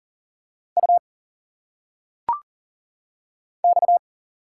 Morse code, heard as a keyed tone, about 40 wpm, 700 Hz. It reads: U X
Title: Single Letters - Phonetic Pronunciation 40wpm